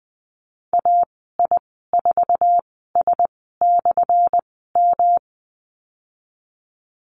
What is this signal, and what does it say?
Morse code, heard as a keyed tone, about 20 wpm, spaced as for 15 wpm, 700 Hz. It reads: AI4S/M
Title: 179 Lesson 29g - Slash - Call Signs 20wpm